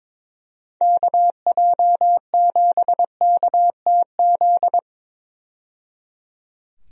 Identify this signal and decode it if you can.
Morse code, heard as a keyed tone, about 22 wpm, 700 Hz. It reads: KJ7KTZ
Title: USA Call Signs - 6 Characters - 22wpm